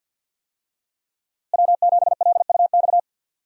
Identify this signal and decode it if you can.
Morse code, heard as a keyed tone, about 47 wpm, 700 Hz. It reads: W7LUX
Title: USA Call Signs - 5 Characters - Repeated 3x - 50wpm